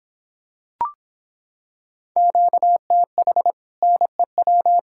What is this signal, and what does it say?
Morse code, heard as a keyed tone, about 26 wpm, 700 Hz. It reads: QTH NEW